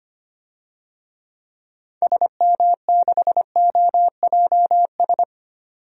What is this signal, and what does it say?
Morse code, heard as a keyed tone, about 25 wpm, 700 Hz. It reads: SM6OJS
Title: DXCC-100 Call Signs - 6 Characters - 25wpm